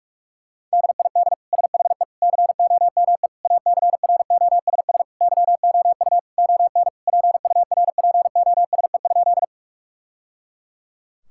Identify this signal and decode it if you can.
Morse code, heard as a keyed tone, about 42 wpm, 700 Hz. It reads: DID SHE COME ACROSS YOU ON PURPOSE?
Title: Sentences from Top 1000 Words - Review Part 5 - 45wpm